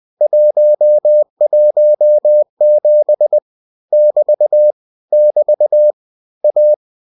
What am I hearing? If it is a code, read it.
Morse code, heard as a keyed tone, about 20 wpm, 600 Hz. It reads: 117 <BT> <BT> A
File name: cw-books / winnie-the-pooh / winnie-20wpm-page-0117